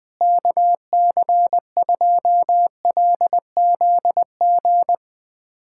Morse code, heard as a keyed tone, about 21 wpm, 700 Hz. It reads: KC2LZG
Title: USA Callsign DE Callsign 20wpm